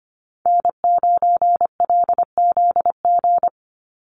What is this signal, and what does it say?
Morse code, heard as a keyed tone, about 25 wpm, 700 Hz. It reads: N9LZG
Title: USA Call Signs 25wpm